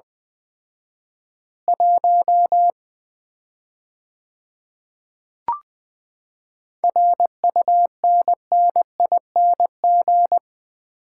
Morse code, heard as a keyed tone, about 20 wpm, 700 Hz. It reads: 1 RUNNING
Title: Common QSO Phrases - 4x Word-Spacing - 20wpm